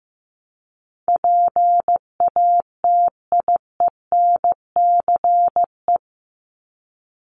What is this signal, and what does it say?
Morse code, heard as a keyed tone, about 15 wpm, 700 Hz. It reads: PATIENCE
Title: Top 1251-1500 Words 15wpm